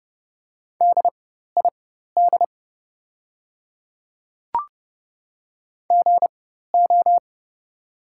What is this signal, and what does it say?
Morse code, heard as a keyed tone, about 30 wpm, spaced as for 14 wpm, 700 Hz. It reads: DID GO
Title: 4x Character Spacing - Top 300 Words - 30wpm